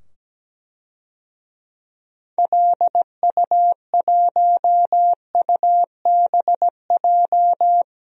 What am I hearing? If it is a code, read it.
Morse code, heard as a keyed tone, about 17 wpm, 700 Hz. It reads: LU1UBJ
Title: DXCC-100 Call Signs - 6 Characters - 17wpm